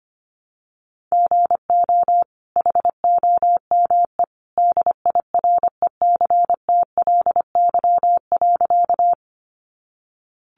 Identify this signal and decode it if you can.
Morse code, heard as a keyed tone, about 25 wpm, 700 Hz. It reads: GO HOME DIRECTLY.
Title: Sentences from Top 800 Words 25wpm